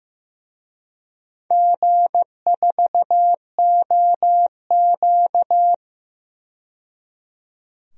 Morse code, heard as a keyed tone, about 15 wpm, 700 Hz. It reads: G4OQ
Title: DXCC-100 Call Signs - 4 Characters - 15wpm